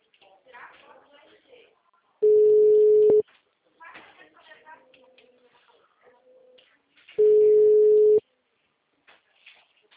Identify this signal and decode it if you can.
Ringback tone (Europe, 425 Hz), with speech.
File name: out- -2023- - - .59257